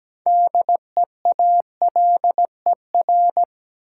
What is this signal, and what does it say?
Morse code, heard as a keyed tone, about 17 wpm, 700 Hz. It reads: DEALER
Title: Sentences from Top 1100 Words - New Word Drill Alt - 17wpm